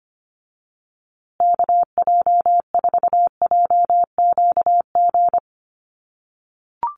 Morse code, heard as a keyed tone, about 25 wpm, 700 Hz. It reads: KJ4JQG
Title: USA Call Signs Repeated 3x 25wpm